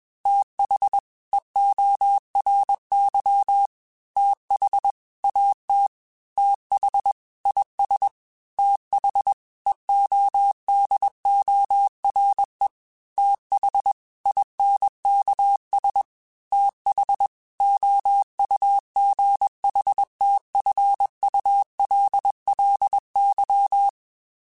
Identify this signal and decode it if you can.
Morse code, heard as a keyed tone, about 21 wpm, 800 Hz. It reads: TH EORY TH AT TH IS TH EODORE TH INKS TH OUGHTFULLY